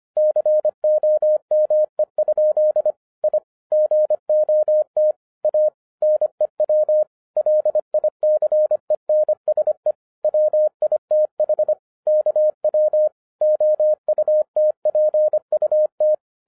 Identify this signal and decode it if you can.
Morse code, heard as a keyed tone, about 25 wpm, 610 Hz. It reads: COME? I GOT A NEW LICENSE WITH KW OUTPUT